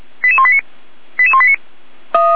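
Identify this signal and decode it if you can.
Five-tone selective call (ZVEI), heard as quick runs of five tones, tones of about 80 ms. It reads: 88189 88189
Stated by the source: SDR reception of ZVEI five-tone call 88166.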